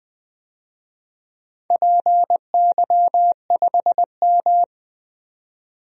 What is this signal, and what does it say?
Morse code, heard as a keyed tone, about 20 wpm, 700 Hz. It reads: PY5M
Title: DXCC-100 Call Signs - 4 Characters - 20wpm